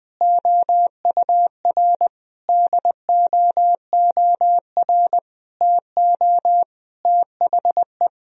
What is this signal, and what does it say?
Morse code, heard as a keyed tone, about 21 wpm, 700 Hz. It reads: OUR DOOR TO THE
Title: Sentences from Top 1000 Words - Review Part 1 - 20wpm